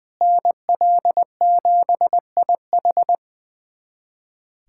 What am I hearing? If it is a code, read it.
Morse code, heard as a keyed tone, about 20 wpm, 700 Hz. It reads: NL7IH
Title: USA Callsign DE Callsign 20wpm